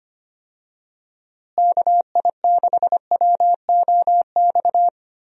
Morse code, heard as a keyed tone, about 25 wpm, 700 Hz. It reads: KI6WOX
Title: USA Call Signs - 6 Characters - Repeated 2x - 25wpm